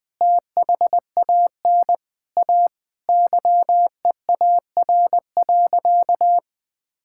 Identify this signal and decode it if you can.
Morse code, heard as a keyed tone, about 20 wpm, 700 Hz. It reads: THAN A YEAR.
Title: Sentences with 5-Word Phrases - 20wpm